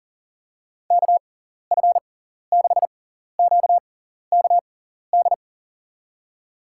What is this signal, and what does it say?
Morse code, heard as a keyed tone, about 40 wpm, spaced as for 14 wpm, 700 Hz. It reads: KF6QKD
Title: USA Call Signs - 6 Characters - 6x Character Spacing - 40wpm